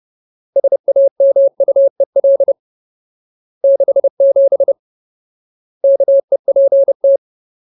Morse code, heard as a keyed tone, about 30 wpm, 550 Hz. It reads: SAMUEL 67 KEPT
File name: Ultimate Word List / 30WPM Ultimate